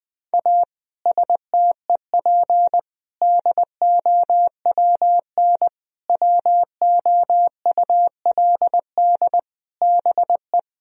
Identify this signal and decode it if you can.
Morse code, heard as a keyed tone, about 20 wpm, 700 Hz. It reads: A STEP DOWN WOULD BE
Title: Sentences from Top 1000 Words - Review Part 4 - 20wpm